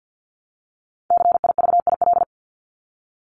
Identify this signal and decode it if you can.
Morse code, heard as a keyed tone, about 50 wpm, 700 Hz. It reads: CIVIL
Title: Top 1751-2000 Words 50wpm